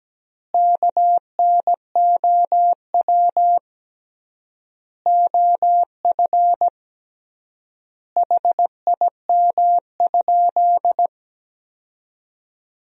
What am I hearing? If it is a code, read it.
Morse code, heard as a keyed tone, about 17 wpm, 700 Hz. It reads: KNOW OF HIM?
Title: 3x Word-Spacing - Sentences from Top 100 Words - 17wpm